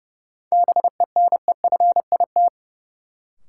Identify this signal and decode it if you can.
Morse code, heard as a keyed tone, about 30 wpm, 700 Hz. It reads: BENEFIT
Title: Sentences from Top 1000 Words - Review Part 6 - New Word Drill - 30wpm